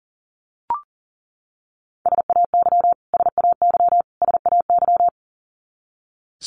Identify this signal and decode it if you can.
Morse code, heard as a keyed tone, about 40 wpm, 700 Hz. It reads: SAY SAY SAY
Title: Top 200 Words Repeated 40wpm